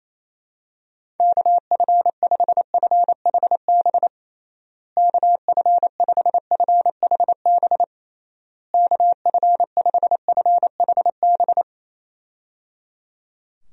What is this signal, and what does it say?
Morse code, heard as a keyed tone, about 28 wpm, 700 Hz. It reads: KF5FHB KF5FHB KF5FHB
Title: USA Call Signs - 6 Characters - Repeated 3x - 28wpm